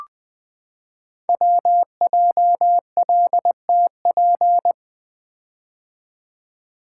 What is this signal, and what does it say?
Morse code, heard as a keyed tone, about 20 wpm, 700 Hz. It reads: WJLTP
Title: Code Groups - 5 Letters - 20wpm